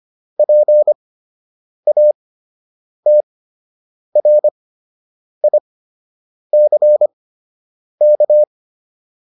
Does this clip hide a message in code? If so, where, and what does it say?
Yes: Morse code from 0.4 to 8.4 s, heard as a keyed tone, about 25 wpm, spaced as for 8 wpm, 600 Hz. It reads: PATRICK